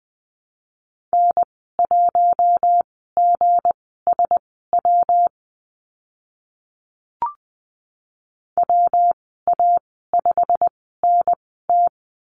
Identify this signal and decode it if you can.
Morse code, heard as a keyed tone, about 20 wpm, spaced as for 15 wpm, 700 Hz. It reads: N1GSW WA5NT
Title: 167 Lesson 27g - Number 7 - Call Signs 20wpm